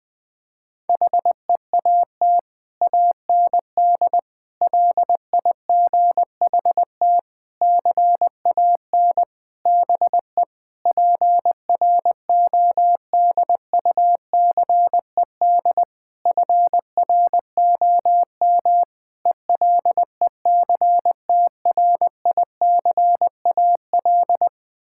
Morse code, heard as a keyed tone, about 21 wpm, 700 Hz. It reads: HEAT AND LIGHT CAN BE PRODUCED FROM ELECTRICAL
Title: Sentences from Top 1000 Words - Review Part 6 - 20wpm